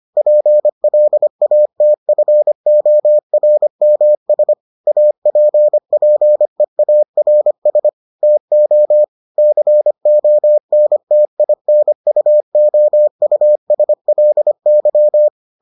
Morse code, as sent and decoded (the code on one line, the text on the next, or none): .--. .-.. .- - ..-. --- .-. -- ... / .- .--. .--. .- .-. ... / - --- / -.-. --- -. - .. -. ..- --- ..- ... .-.. -.--
PLATFORMS APPEARS TO CONTINUOUSLY